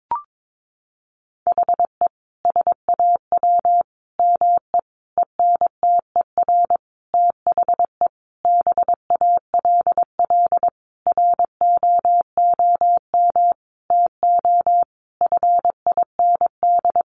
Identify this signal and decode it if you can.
Morse code, heard as a keyed tone, about 22 wpm, 700 Hz. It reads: HE SAW ME ENTER THE BALL ROOM TO FIND
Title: Sentences from Top 1300 Words 22wpm